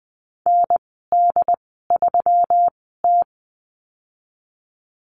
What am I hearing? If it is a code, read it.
Morse code, heard as a keyed tone, about 20 wpm, spaced as for 15 wpm, 700 Hz. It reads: ND3T
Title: 122 Lesson 20g - Number 3 - Call Signs 20wpm